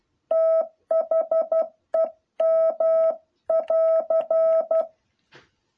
Morse code, heard as a keyed tone, about 12 wpm, 650 Hz. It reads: THEM<AR>
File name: jh8cbh / 2021cw